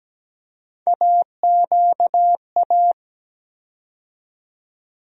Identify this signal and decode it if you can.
Morse code, heard as a keyed tone, about 17 wpm, 700 Hz. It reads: AQA
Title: USA Call Sign Postfixes - 3 Letters 17wpm